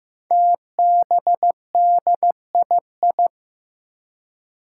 Morse code, heard as a keyed tone, about 15 wpm, 700 Hz. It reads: TBDII
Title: Five Challenging Letter-Number Combinations 15wpm